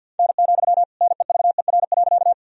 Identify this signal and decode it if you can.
Morse code, heard as a keyed tone, about 52 wpm, 700 Hz. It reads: N, NEVER.
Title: Sentences from Top 300 Words and Abbreviations 50wpm